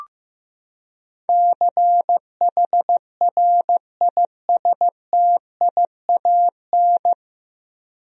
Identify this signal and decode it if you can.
Morse code, heard as a keyed tone, about 15 wpm, 700 Hz. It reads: CHRISTIAN
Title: USA Common Names - 15wpm